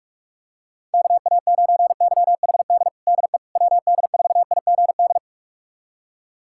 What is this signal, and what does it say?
Morse code, heard as a keyed tone, about 42 wpm, 700 Hz. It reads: KA9YHD DE WD4IGD
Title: USA Callsign DE Callsign 45wpm